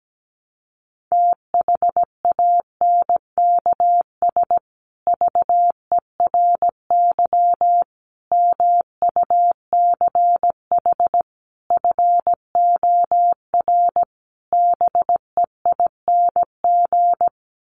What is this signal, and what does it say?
Morse code, heard as a keyed tone, about 17 wpm, 700 Hz. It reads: THANKS VERY MUCH FOR BEING
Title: Sets of 5 Words Encore 17wpm